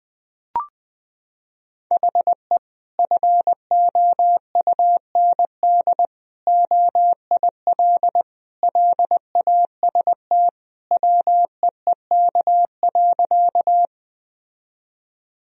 Morse code, heard as a keyed tone, about 20 wpm, 700 Hz. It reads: HE FOUND OIL LAST WEEK.
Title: Sentences from Top 1000 Words - Review Part 4 - 20wpm